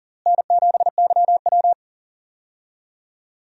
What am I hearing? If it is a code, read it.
Morse code, heard as a keyed tone, about 40 wpm, 700 Hz. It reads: N7YW
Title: USA Call Signs - 4 Characters - Repeated 3x - 40wpm